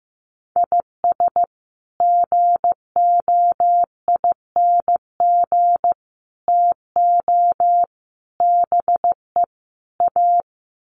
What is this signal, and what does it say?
Morse code, heard as a keyed tone, about 15 wpm, 700 Hz. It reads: IS GOING TO BE A
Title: Sets of 5 Words 15wpm